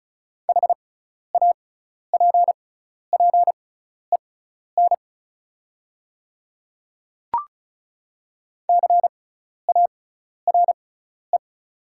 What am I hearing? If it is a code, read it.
Morse code, heard as a keyed tone, about 36 wpm, spaced as for 12 wpm, 690 Hz. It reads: HAPPEN CARE
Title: 6x Character Spacing - Top 500 Words - 35wpm